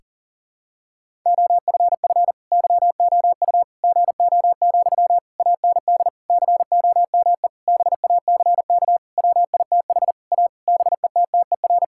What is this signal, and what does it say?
Morse code, heard as a keyed tone, about 40 wpm, 700 Hz. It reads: OFF YOU GO, AND COME BACK WITH A BETTER
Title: Sentences from Top 1000 Words - Review Part 6 - 40wpm